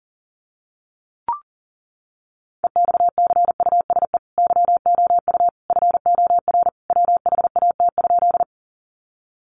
Morse code, heard as a keyed tone, about 40 wpm, 700 Hz. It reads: EXCUSE YOU FOR WHAT?
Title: Sentences from Top 700 Words 40wpm